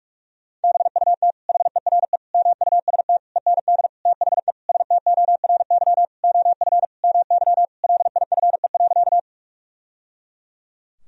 Morse code, heard as a keyed tone, about 46 wpm, 700 Hz. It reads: BUT HERE MUST END THE STORY OF MY LIFE.
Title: Sentences from Top 1000 Words - Review Part 1 - 45wpm